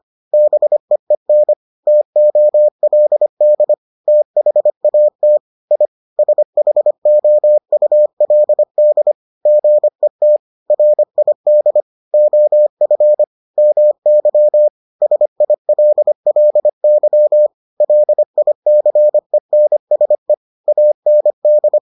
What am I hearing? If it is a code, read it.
Morse code, heard as a keyed tone, about 25 wpm, 600 Hz. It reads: BEEN TOLD THAT I SHOULD GET RID OF MY SILLY LICENSE AND